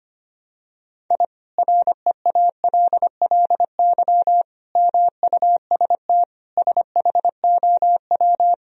Morse code, heard as a keyed tone, about 25 wpm, 700 Hz. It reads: I REALLY MUST SHOW
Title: Sentences from Top 1000 Words - Review Part 4 - 25wpm